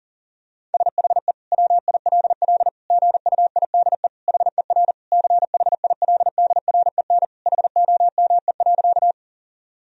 Morse code, heard as a keyed tone, about 40 wpm, 700 Hz. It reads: SHE WILL GUIDE HER CHILDREN HOME.
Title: Sentences from Top 1000 Words - Review Part 3 - 40wpm